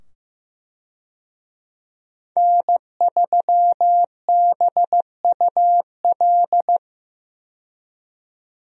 Morse code, heard as a keyed tone, about 15 wpm, 700 Hz. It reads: N3BUL
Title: USA Call Signs - 5 Characters - 15wpm